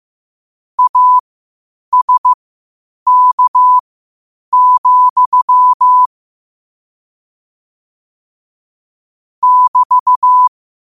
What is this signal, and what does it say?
Morse code, heard as a keyed tone, about 15 wpm, spaced as for 8 wpm, 1000 Hz. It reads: ASK, <BT>
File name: qotdfiles / 8wpm